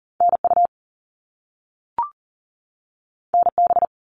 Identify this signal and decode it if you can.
Morse code, heard as a keyed tone, about 40 wpm, 700 Hz. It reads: NU NB
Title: Canadian Provinces and Territories 40wpm